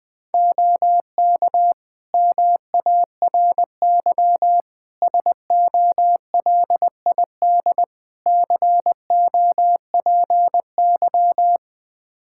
Morse code, heard as a keyed tone, about 20 wpm, 700 Hz. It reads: OK MARY SOLID COPY
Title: Practice QSO - Exchange 3 - More Info - 20wpm